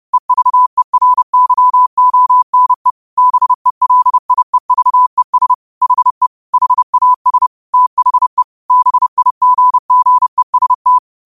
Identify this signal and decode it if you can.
Morse code, heard as a keyed tone, about 30 wpm, 1000 Hz. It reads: EVERYONE BELIEVES HE HAS THE BIGGEST
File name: qotdfiles / 30wpm